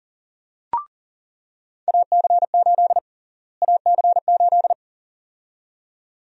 Morse code, heard as a keyed tone, about 40 wpm, 700 Hz. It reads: AC8 AC8
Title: USA Call Sign Prefixes - Repeated 2x - 40wpm